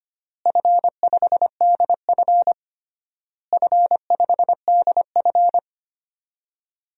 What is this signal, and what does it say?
Morse code, heard as a keyed tone, about 25 wpm, 700 Hz. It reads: F5DF F5DF
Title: DXCC-100 Call Signs - 4 Characters - Repeated 3x - 25wpm